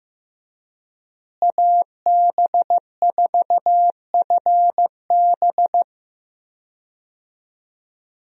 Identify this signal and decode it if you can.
Morse code, heard as a keyed tone, about 14 wpm, 700 Hz. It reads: AB4FB
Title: USA Call Signs - 5 Characters - Repeated 3x - 15wpm